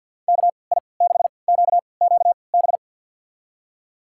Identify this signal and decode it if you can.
Morse code, heard as a keyed tone, about 48 wpm, spaced as for 28 wpm, 700 Hz. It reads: KI6QQB
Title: USA Call Signs - 6 Characters - 3x Character Spacing - 50wpm